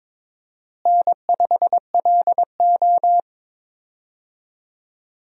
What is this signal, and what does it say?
Morse code, heard as a keyed tone, about 22 wpm, 700 Hz. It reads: N5LO
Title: USA Call Signs - 4 Characters - Repeated 2x - 22wpm